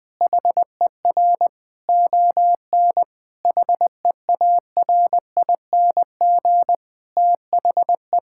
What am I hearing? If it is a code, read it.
Morse code, heard as a keyed tone, about 20 wpm, 700 Hz. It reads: HER ON HEARING THE
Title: Difficult Sentences with Hints - 20wpm